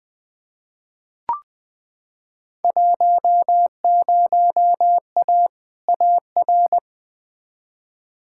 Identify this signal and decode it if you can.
Morse code, heard as a keyed tone, about 20 wpm, 700 Hz. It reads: 10A AR
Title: Field Day Exchanges Encore - 20wpm